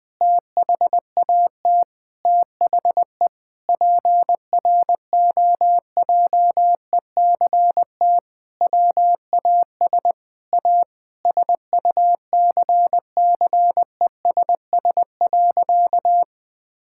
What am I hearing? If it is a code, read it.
Morse code, heard as a keyed tone, about 20 wpm, 700 Hz. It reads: THAT THE PROJECT WAS A SUCCESS.
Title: Sentences with 9-Word Phrases - 20wpm